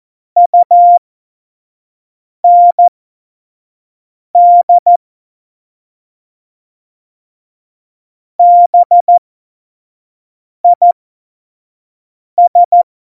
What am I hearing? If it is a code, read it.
Morse code, heard as a keyed tone, about 14 wpm, spaced as for 5 wpm, 700 Hz. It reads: UND BIS
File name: Morsekurs / Witze / Witz 12. September 17, Tempo 25